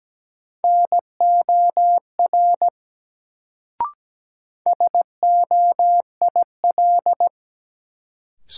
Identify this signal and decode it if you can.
Morse code, heard as a keyed tone, about 17 wpm, 700 Hz. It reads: NOR SOIL
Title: Sentences from Top 1500 Words - New Word Drill Alt - 17wpm